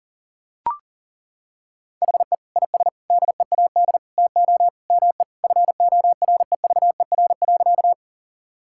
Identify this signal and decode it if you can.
Morse code, heard as a keyed tone, about 37 wpm, 700 Hz. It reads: HE IS DEAD TO ME FOREVER.
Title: Sentences from Top 1600 Words 40wpm